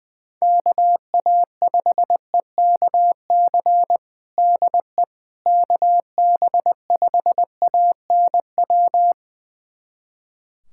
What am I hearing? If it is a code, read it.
Morse code, heard as a keyed tone, about 20 wpm, 700 Hz. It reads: KA5EKC DE KB5ANW
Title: USA Callsign DE Callsign - 6 Characters - 20wpm